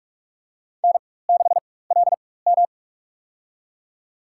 Morse code, heard as a keyed tone, about 46 wpm, spaced as for 21 wpm, 700 Hz. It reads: N6LM
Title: USA Call Signs - 4 Characters - 4x Character Spacing - 45wpm